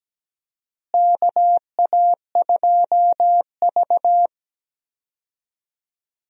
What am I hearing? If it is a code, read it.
Morse code, heard as a keyed tone, about 17 wpm, 700 Hz. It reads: KA2V
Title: USA Call Signs - 4 Characters - Repeated 3x - 17wpm